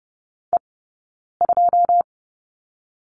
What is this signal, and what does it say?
Morse code, heard as a keyed tone, about 30 wpm, spaced as for 9 wpm, 690 Hz. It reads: E2
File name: course / 30:10wpm / 086 Lesson 15d - Number 5 - Two Characters 30wpm